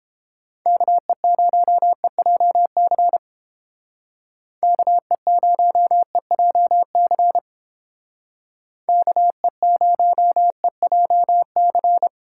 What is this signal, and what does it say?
Morse code, heard as a keyed tone, about 29 wpm, 700 Hz. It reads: KE0EJC KE0EJC KE0EJC